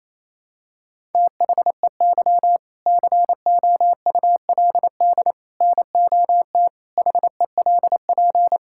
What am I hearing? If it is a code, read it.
Morse code, heard as a keyed tone, about 27 wpm, 700 Hz. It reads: THEY COULD NOT HELP
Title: Sentences from Top 1000 Words - Review Part 5 - 28wpm